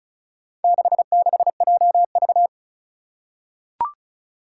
Morse code, heard as a keyed tone, about 36 wpm, 700 Hz. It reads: 66JV
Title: Four Challenging Letter-Number Combinations 35wpm